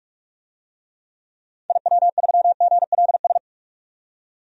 Morse code, heard as a keyed tone, about 43 wpm, 700 Hz. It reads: IW3GLS
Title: DXCC-100 Call Signs - 6 Characters - 45wpm